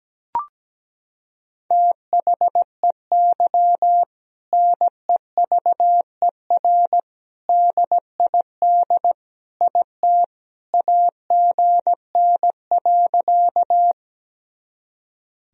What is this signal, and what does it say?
Morse code, heard as a keyed tone, about 17 wpm, 700 Hz. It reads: THEY NEVER DID IT AGN.
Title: Sentences from Top 300 Words and Abbreviations 17wpm